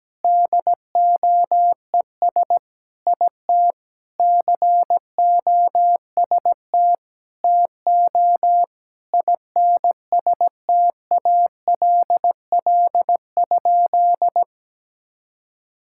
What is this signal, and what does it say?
Morse code, heard as a keyed tone, about 17 wpm, 700 Hz. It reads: DOES IT COST TO INSTALL?
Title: Sentences from Top 1600 Words 17wpm